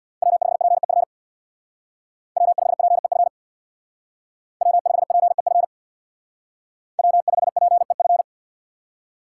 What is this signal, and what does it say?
Morse code, heard as a keyed tone, about 64 wpm, 700 Hz. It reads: W5PEF W5PEF W5PEF W5PEF